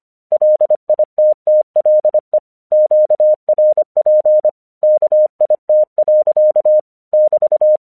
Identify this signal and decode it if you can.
Morse code, heard as a keyed tone, about 24 wpm, 610 Hz. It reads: LITTLE QRP KIT. <BT>
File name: 1 English 25wpm / AQC20